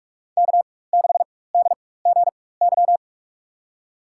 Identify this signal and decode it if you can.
Morse code, heard as a keyed tone, about 46 wpm, spaced as for 21 wpm, 700 Hz. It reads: K6DGY
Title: USA Call Signs - 5 Characters - 4x Character Spacing - 45wpm